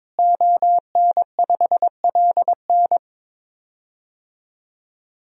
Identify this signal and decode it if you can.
Morse code, heard as a keyed tone, about 22 wpm, 700 Hz. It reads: ON5LN
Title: DXCC-100 Call Signs - 5 Characters - Repeated 2x - 22wpm